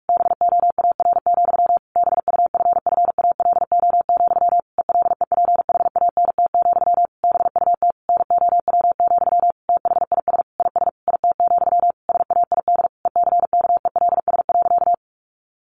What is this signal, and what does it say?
Morse code, heard as a keyed tone, about 44 wpm, 700 Hz. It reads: BOAR, BUFFALO, ELEPHANT, BUT NOW, THIS IS IT, SAID ECKELS.
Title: A Sound Of Thunder 45wpm